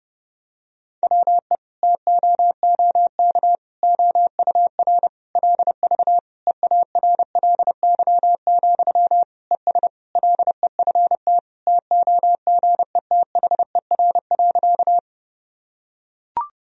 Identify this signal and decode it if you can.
Morse code, heard as a keyed tone, about 30 wpm, 700 Hz. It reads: WE TOOK OUR LV EARLY, ES LEFT TOGETHER.
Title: Sentences from Top 400 Words and Abbreviations 30wpm